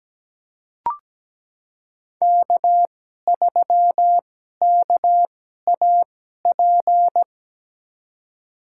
Morse code, heard as a keyed tone, about 17 wpm, spaced as for 12 wpm, 700 Hz. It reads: K3KAP
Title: USA Call Signs - 5 Characters - 2x Character Spacing - 17wpm